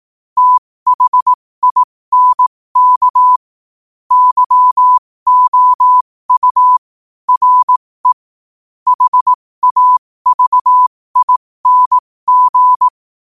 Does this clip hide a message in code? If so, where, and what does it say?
Yes: Morse code from 0.4 to 12.9 s, heard as a keyed tone, about 18 wpm, spaced as for 15 wpm, 1000 Hz. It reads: THINK YOU RE HAVING